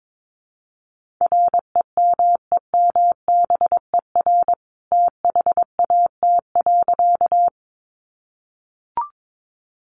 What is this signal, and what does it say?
Morse code, heard as a keyed tone, about 22 wpm, 700 Hz. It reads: REMEMBER THAT.
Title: A Sound Of Thunder 22wpm